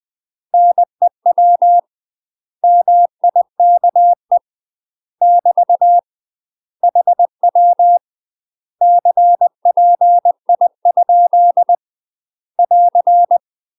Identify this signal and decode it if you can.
Morse code, heard as a keyed tone, about 20 wpm, 700 Hz. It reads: NEW MIKE <BT> HW CPI? <AR>
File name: UK MASTER QSOs TRAINING / 20WPM / UK MASTER QSOs 0060